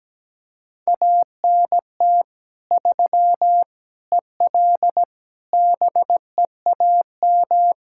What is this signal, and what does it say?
Morse code, heard as a keyed tone, about 17 wpm, 700 Hz. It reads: ANT 3 EL BEAM
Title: Practice QSO - Exchange 2 - Info - 17wpm